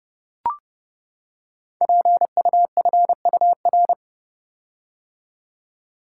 Morse code, heard as a keyed tone, about 28 wpm, 700 Hz. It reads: PUFUR
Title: Code Groups - 5 Letters - 30wpm